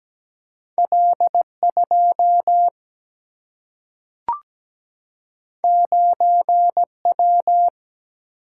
Morse code, heard as a keyed tone, about 17 wpm, 700 Hz. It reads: L 2 9 W
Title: Single-Letter Single-Number 17wpm